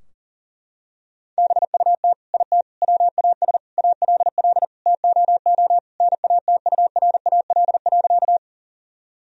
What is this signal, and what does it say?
Morse code, heard as a keyed tone, about 40 wpm, 700 Hz. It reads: BUT IT WAS ALL TOO NATURAL.
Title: Sentences from Top 1000 Words - Review Part 4 - 40wpm